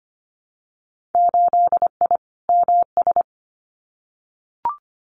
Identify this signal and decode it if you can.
Morse code, heard as a keyed tone, about 25 wpm, 700 Hz. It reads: 8I MS
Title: Winter Field Day Exchanges Encore 25wpm